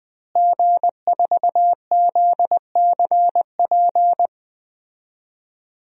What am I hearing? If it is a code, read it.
Morse code, heard as a keyed tone, about 21 wpm, 700 Hz. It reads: G4ZCP
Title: DXCC-100 Call Signs - 5 Characters - Repeated 2x - 20wpm